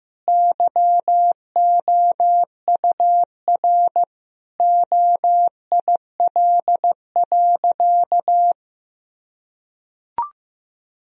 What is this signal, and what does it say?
Morse code, heard as a keyed tone, about 16 wpm, 700 Hz. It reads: YOUR OIL.
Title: Sentences from Top 1000 Words - Review Part 4 - 15wpm